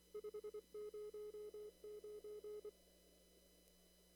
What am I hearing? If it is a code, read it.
Morse code, heard as a keyed tone, about 24 wpm, 420 Hz. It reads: 509
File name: CW / kt5j wpx cw-op-k5pi